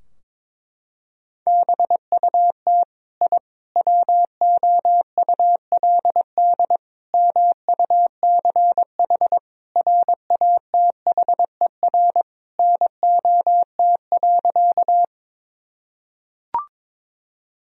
Morse code, heard as a keyed tone, about 21 wpm, 700 Hz. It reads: BUT I WOULD MUCH RATHER NOT.
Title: Sentences from Top 1000 Words - Review Part 4 - 22wpm